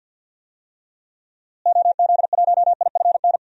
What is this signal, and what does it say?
Morse code, heard as a keyed tone, about 48 wpm, 700 Hz. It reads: OZ1IFN
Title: DXCC-100 Call Signs - 6 Characters - Repeated 3x - 50wpm